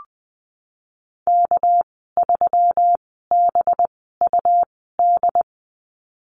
Morse code, heard as a keyed tone, about 20 wpm, spaced as for 15 wpm, 700 Hz. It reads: K3BUD
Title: 219 Lesson 35g - Number 0 - Call Signs 20wpm